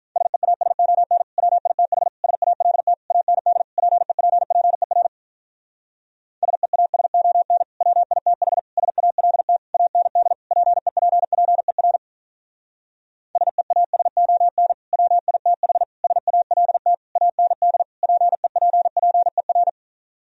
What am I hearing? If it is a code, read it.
Morse code, heard as a keyed tone, about 46 wpm, 700 Hz. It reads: SEASON WITH SALT AND PEPPER SEASON WITH SALT AND PEPPER SEASON WITH SALT AND PEPPER